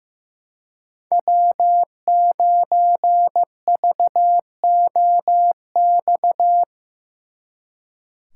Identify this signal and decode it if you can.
Morse code, heard as a keyed tone, about 18 wpm, 700 Hz. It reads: W9VOX
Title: Speed Racing - USA 1x3 Call Signs 15wpm